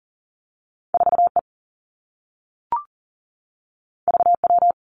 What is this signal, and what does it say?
Morse code, heard as a keyed tone, about 41 wpm, 700 Hz. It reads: 4 E V W
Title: Two Challenging Letter-Number Combinations 40wpm